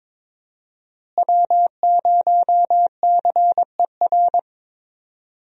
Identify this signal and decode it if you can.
Morse code, heard as a keyed tone, about 22 wpm, 700 Hz. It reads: W0CER
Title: USA Call Signs - 5 Characters - Repeated 3x - 22wpm